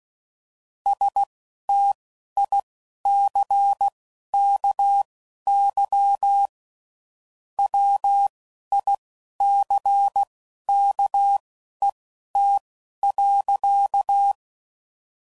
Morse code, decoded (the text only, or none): STICKY WICKET.